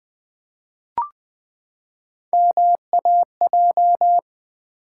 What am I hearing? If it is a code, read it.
Morse code, heard as a keyed tone, about 20 wpm, 700 Hz. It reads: MAJ
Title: Build A Word - 3 Letters At A Time - 8-Letter Words - 20wpm